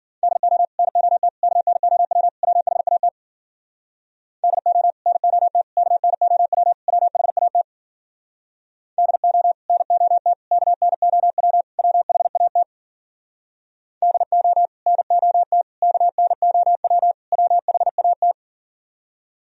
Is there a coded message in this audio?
Yes: Morse code from 0.2 to 18.3 s, heard as a keyed tone, about 52 wpm, 700 Hz. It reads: DO NOT KNOW WHAT DO NOT KNOW WHAT DO NOT KNOW WHAT DO NOT KNOW WHAT